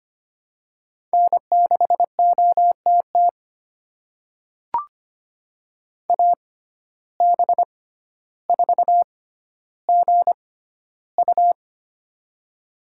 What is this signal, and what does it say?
Morse code, heard as a keyed tone, about 25 wpm, spaced as for 13 wpm, 700 Hz. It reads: N6OTT AB4GU
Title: USA Call Signs - 5 Characters - 6x Character Spacing - 25wpm